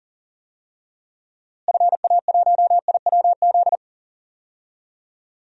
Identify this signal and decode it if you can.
Morse code, heard as a keyed tone, about 37 wpm, 700 Hz. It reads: FA1IWZ
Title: DXCC-100 Call Signs - 6 Characters - Repeated 3x - 40wpm